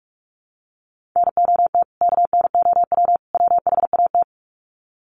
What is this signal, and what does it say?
Morse code, heard as a keyed tone, about 44 wpm, 700 Hz. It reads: NOT KNOW WHAT
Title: Sets of 3 Words 45wpm